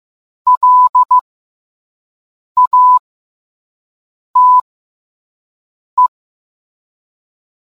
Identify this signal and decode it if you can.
Morse code, heard as a keyed tone, about 15 wpm, spaced as for 5 wpm, 1000 Hz. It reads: LATE